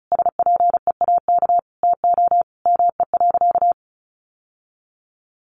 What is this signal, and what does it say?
Morse code, heard as a keyed tone, about 34 wpm, 700 Hz. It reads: SPEAK TO ME.
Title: Sentences from Top 1100 Words 35wpm